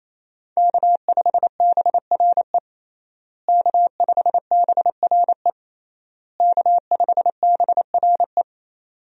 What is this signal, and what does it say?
Morse code, heard as a keyed tone, about 28 wpm, 700 Hz. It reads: K5BRE K5BRE K5BRE
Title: USA Call Signs - 5 Characters - Repeated 3x - 28wpm